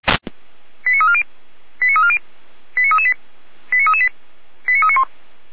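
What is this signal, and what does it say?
Five-tone selective call (ZVEI), heard as quick runs of five tones, tones of about 80 ms. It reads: 89340 89340 89338 89338 89391
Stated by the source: ZVEI five-tone selective call received by SDR, from audio.f85360.feuerwehr-orlamuende.org fire brigade server.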